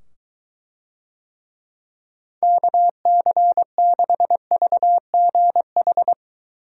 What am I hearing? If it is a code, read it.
Morse code, heard as a keyed tone, about 21 wpm, 700 Hz. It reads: KC6VGH